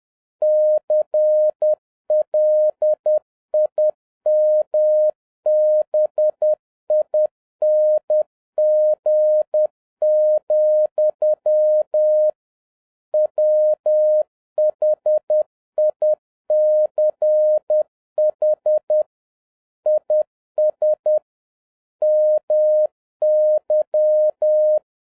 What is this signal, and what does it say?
Morse code, heard as a keyed tone, about 10 wpm, 610 Hz. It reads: CLIMBING, WHICH IS MY